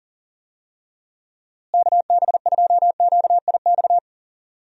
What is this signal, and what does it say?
Morse code, heard as a keyed tone, about 40 wpm, 700 Hz. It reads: KB2QIX
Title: USA Call Signs - 6 Characters - Repeated 3x - 40wpm